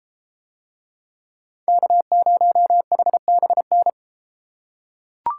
Morse code, heard as a keyed tone, about 32 wpm, 700 Hz. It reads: K0HBN